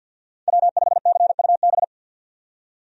Morse code, heard as a keyed tone, about 50 wpm, 700 Hz. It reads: W5CUB
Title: USA Call Signs - 5 Characters - 50wpm